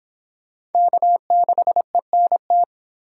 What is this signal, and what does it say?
Morse code, heard as a keyed tone, about 30 wpm, 700 Hz. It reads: K6ENT